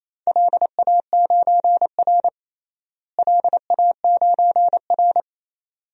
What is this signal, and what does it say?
Morse code, heard as a keyed tone, about 28 wpm, 700 Hz. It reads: LA9R LA9R
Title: DXCC-100 Call Signs - 4 Characters - Repeated 2x - 28wpm